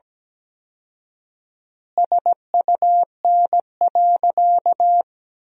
Morse code, heard as a keyed tone, about 17 wpm, 700 Hz. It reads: SUN.
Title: 4x Word-Spacing - Sentences from Top 300 Words - 17wpm